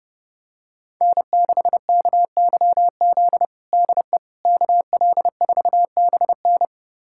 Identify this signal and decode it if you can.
Morse code, heard as a keyed tone, about 32 wpm, 700 Hz. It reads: N6KYZ DE KL4BN
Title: USA Callsign DE Callsign - 5 Characters - 30wpm